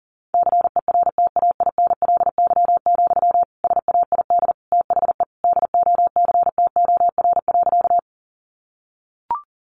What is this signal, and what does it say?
Morse code, heard as a keyed tone, about 40 wpm, 700 Hz. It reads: CERTAINLY, SAID THE DOCTOR.
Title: Sentences from Top 1500 Words 40wpm